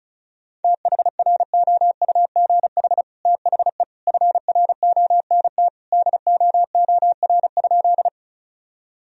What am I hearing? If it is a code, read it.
Morse code, heard as a keyed tone, about 34 wpm, 700 Hz. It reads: THROUGH THE FRONT DOOR?
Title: Sentences from Top 1000 Words - Review Part 2 - 35wpm